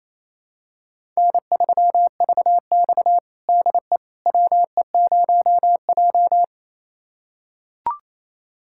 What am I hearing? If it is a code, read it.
Morse code, heard as a keyed tone, about 28 wpm, 700 Hz. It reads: N3VX DE WE0J
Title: USA Callsign DE Callsign - 4 Characters - 28wpm